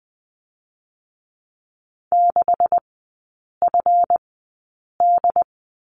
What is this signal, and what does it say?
Morse code, heard as a keyed tone, about 20 wpm, spaced as for 8 wpm, 700 Hz. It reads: 6FD
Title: 145 Lesson 24e - Letter y - Three Characters 20wpm